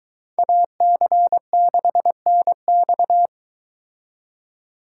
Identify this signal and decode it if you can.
Morse code, heard as a keyed tone, about 22 wpm, 700 Hz. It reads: AC6NX